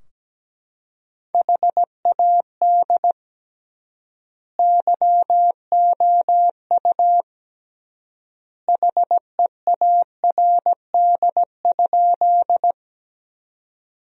Morse code, heard as a keyed tone, about 17 wpm, 700 Hz. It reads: HAD YOU HEARD?
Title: 3x Word-Spacing - Sentences from Top 400 Words - 17wpm